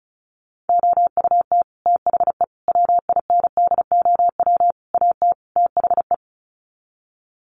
Morse code, heard as a keyed tone, about 34 wpm, 700 Hz. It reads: OUT THE WINDOW AT THE
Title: Sets of 5 Words Encore2 35wpm